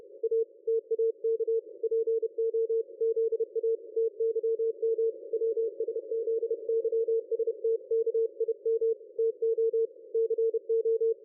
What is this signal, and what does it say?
Morse code, heard as a keyed tone, about 30 wpm, 450 Hz. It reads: A TAK PO ZA TYM WSZYSTKIM TO CO
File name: dlazaawansowanych / nagraniazpasma / ExpertSDR2 line out 23.01.2020 14.22.53